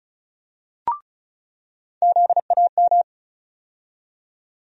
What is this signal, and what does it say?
Morse code, heard as a keyed tone, about 36 wpm, 700 Hz. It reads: ZAM
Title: USA Call Sign Postfixes - 3 Letters 35wpm